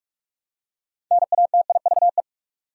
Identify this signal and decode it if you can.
Morse code, heard as a keyed tone, about 46 wpm, 700 Hz. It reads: NATIVE
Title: Six Letter Words - 45wpm